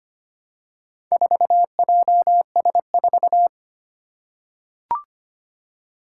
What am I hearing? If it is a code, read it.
Morse code, heard as a keyed tone, about 25 wpm, 700 Hz. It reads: 4JS4
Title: Four Challenging Letter-Number Combinations 25wpm